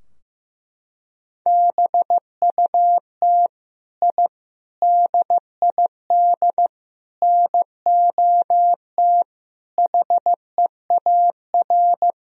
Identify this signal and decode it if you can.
Morse code, heard as a keyed tone, about 15 wpm, 700 Hz. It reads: BUT I DID NOT HEAR
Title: Sentences from Top 1000 Words - Review Part 4 - 15wpm